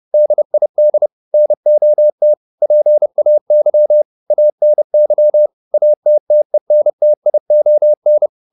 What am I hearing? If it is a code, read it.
Morse code, heard as a keyed tone, about 30 wpm, 600 Hz. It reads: DID NOT PAY ANY ATTENTION